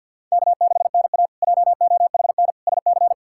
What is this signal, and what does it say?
Morse code, heard as a keyed tone, about 46 wpm, 700 Hz. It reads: K6NA JOHN SC
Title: K1USN SST QSO Practice - 50wpm